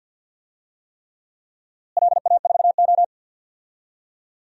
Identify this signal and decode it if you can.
Morse code, heard as a keyed tone, about 50 wpm, 700 Hz. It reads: LA4O
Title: DXCC-100 Call Signs - 4 Characters - Repeated 3x - 50wpm